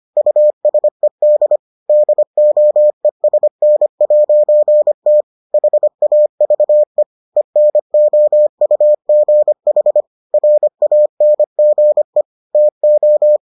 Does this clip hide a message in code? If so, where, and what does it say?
Yes: Morse code from 0.2 to 13.4 s, heard as a keyed tone, about 25 wpm, 600 Hz. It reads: USED DOESN'T HAVE ENOUGH RANGE TO